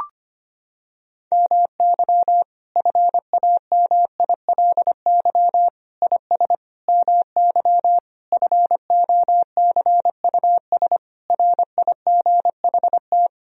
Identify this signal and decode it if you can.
Morse code, heard as a keyed tone, about 25 wpm, 700 Hz. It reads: MY FAMILY IS MY FOCUS RIGHT
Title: Sentences from Top 1000 Words - Review Part 4 - 25wpm